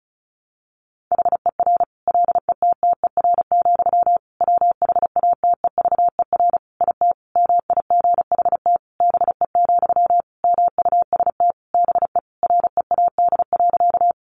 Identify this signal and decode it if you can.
Morse code, heard as a keyed tone, about 34 wpm, 700 Hz. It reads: HER LETTER, WHATEVER IT MIGHT BE, MUST BE READ.
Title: Sentences from Top 1100 Words 35wpm